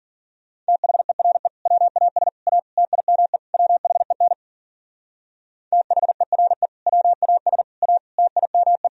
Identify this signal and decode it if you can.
Morse code, heard as a keyed tone, about 54 wpm, 700 Hz. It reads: THERE WAS A TIME WHEN THERE WAS A TIME